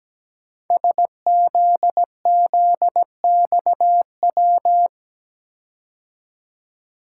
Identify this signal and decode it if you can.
Morse code, heard as a keyed tone, about 17 wpm, 700 Hz. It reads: SZZXW
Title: Code Groups - 5 Letters - 17wpm